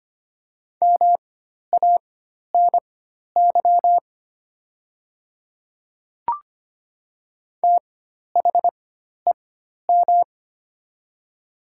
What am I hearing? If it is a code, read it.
Morse code, heard as a keyed tone, about 25 wpm, spaced as for 12 wpm, 700 Hz. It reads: MANY THEM
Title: 4x Character Spacing - Top 100 Words - 25wpm